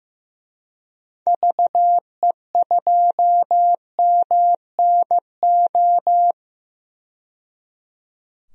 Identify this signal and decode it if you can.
Morse code, heard as a keyed tone, about 15 wpm, 700 Hz. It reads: VE2MNO
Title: DXCC-100 Call Signs - 6 Characters - 15wpm